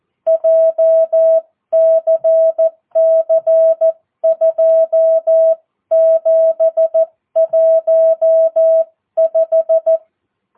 Morse code, heard as a keyed tone, about 14 wpm, 650 Hz. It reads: JCC 2715
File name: jh8cbh / 2022cw